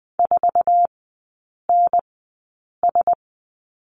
Morse code, heard as a keyed tone, about 20 wpm, spaced as for 8 wpm, 700 Hz. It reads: 4NS
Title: 067 Lesson 12e - Letter d - Three Characters 20wpm